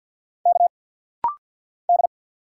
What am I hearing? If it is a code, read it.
Morse code, heard as a keyed tone, about 49 wpm, 690 Hz. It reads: K D
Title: Single Letter-Number - Rapid-Fire 50wpm